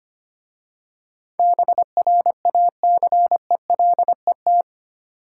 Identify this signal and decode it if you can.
Morse code, heard as a keyed tone, about 25 wpm, 700 Hz. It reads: BRACELET
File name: practice / 25wpm / Easy Words with Hints - 25wpm